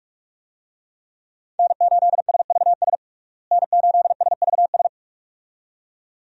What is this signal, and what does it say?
Morse code, heard as a keyed tone, about 43 wpm, 700 Hz. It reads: N8SVS N8SVS
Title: USA Call Signs - 5 Characters - Repeated 2x - 45wpm